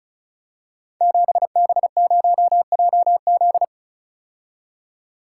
Morse code, heard as a keyed tone, about 36 wpm, 700 Hz. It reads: 7B0JZ
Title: Code Groups - 5 Letter:Numbers - 35wpm